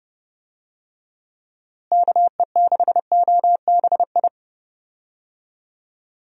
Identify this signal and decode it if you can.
Morse code, heard as a keyed tone, about 30 wpm, 700 Hz. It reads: KE6OBI
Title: USA Call Signs - 6 Characters - Repeated 3x - 30wpm